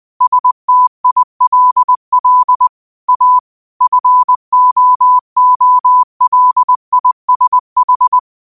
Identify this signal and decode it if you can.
Morse code, heard as a keyed tone, about 20 wpm, 1000 Hz. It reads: STILL A FOOLISH
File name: qotdfiles / 20wpm